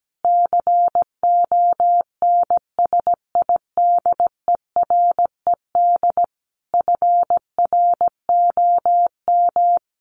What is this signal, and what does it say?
Morse code, heard as a keyed tone, about 17 wpm, 700 Hz. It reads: CONSIDERED FROM
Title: Sets of 5 Words Encore 17wpm